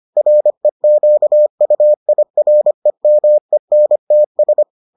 Morse code, heard as a keyed tone, about 25 wpm, 600 Hz. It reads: REQUIREMENTS